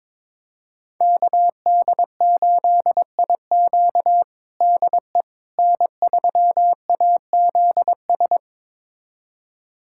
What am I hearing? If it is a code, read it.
Morse code, heard as a keyed tone, about 22 wpm, 700 Hz. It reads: KD8IQ DE N3AZS
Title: USA Callsign DE Callsign - 5 Characters - 22wpm